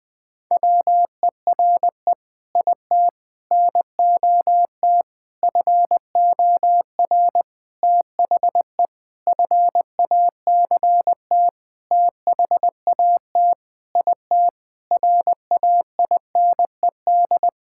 Morse code, heard as a keyed tone, about 20 wpm, 700 Hz. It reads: WERE IT NOT FOR THE FACT THAT IT RAINED
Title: Sentences with 7-Word Phrases - Spoken Beginning - 20wpm